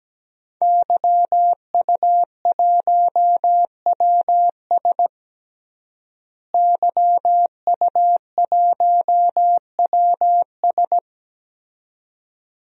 Morse code, heard as a keyed tone, about 17 wpm, 700 Hz. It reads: YU1WS YU1WS
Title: DXCC-100 Call Signs - 5 Characters - Repeated 2x - 17wpm